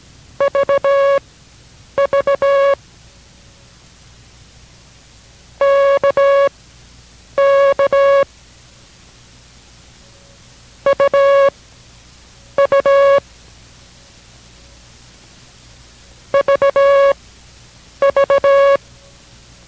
Morse code, heard as a keyed tone, about 17 wpm, spaced as for 8 wpm, 560 Hz. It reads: VV KK UU VV